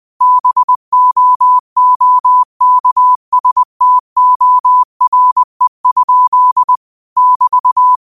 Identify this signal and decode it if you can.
Morse code, heard as a keyed tone, about 20 wpm, 1000 Hz. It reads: BOOKSTORE? <BT>
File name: qotdfiles / 20wpm